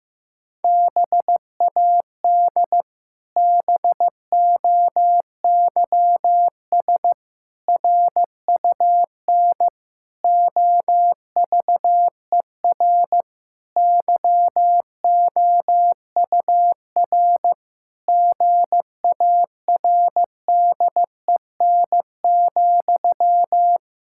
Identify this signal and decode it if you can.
Morse code, heard as a keyed tone, about 16 wpm, 700 Hz. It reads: BAD BOYS RUN OVER YOUR GARDEN,
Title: Pangram Challenge v2 - 15wpm